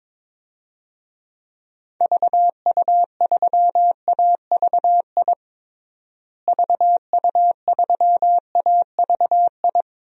Morse code, heard as a keyed tone, about 22 wpm, 700 Hz. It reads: VU3AVI VU3AVI
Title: DXCC-100 Call Signs - 6 Characters - Repeated 2x - 22wpm